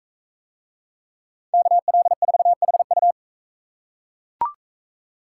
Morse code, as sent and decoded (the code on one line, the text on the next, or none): -.- .-.. ....- .... ..-
KL4HU